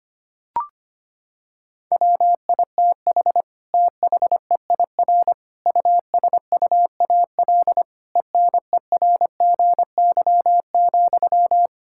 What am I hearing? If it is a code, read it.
Morse code, heard as a keyed tone, about 25 wpm, 700 Hz. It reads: WITH THEIR USUAL ENERGY,
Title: Sentences from Top 1000 Words - Review Part 6 - 25wpm